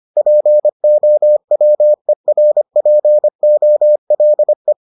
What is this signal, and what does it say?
Morse code, heard as a keyed tone, about 25 wpm, 600 Hz. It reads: POWERPOLE